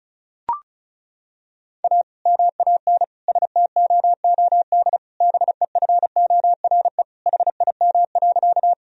Morse code, heard as a keyed tone, about 36 wpm, 700 Hz. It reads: A MAN STOOD BEFORE HIM.
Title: Sentences from Top 1000 Words - Review Part 2 - 35wpm